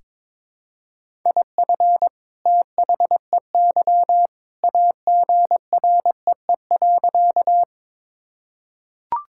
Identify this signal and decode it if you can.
Morse code, heard as a keyed tone, about 22 wpm, 700 Hz. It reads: IF THEY AGREE.
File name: practice / 22wpm / Sentences with 2-Word Phrases - 22wpm